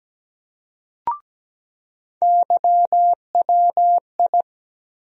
Morse code, heard as a keyed tone, about 17 wpm, 700 Hz. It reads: YWI
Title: USA Call Sign Postfixes - 3 Letters 17wpm